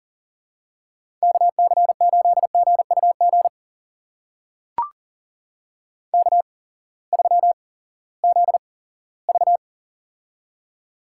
Morse code, heard as a keyed tone, about 40 wpm, spaced as for 20 wpm, 700 Hz. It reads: KC8GUG K3ZV
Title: USA Call Signs - 8x Character Spacing - 40wpm